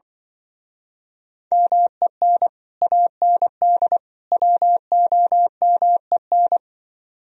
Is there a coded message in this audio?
Yes: Morse code from 1.5 to 6.6 s, heard as a keyed tone, about 25 wpm, 700 Hz. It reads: MEN AND WOMEN